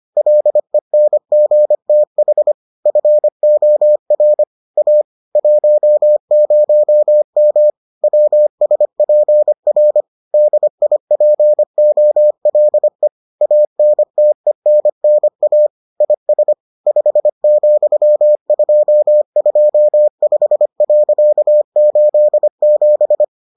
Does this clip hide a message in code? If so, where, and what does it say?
Yes: Morse code from 0.2 to 23.2 s, heard as a keyed tone, about 25 wpm, 600 Hz. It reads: LENGTH FOR A 10M WSPR DIPOLE ANTENNA IS 5,225.87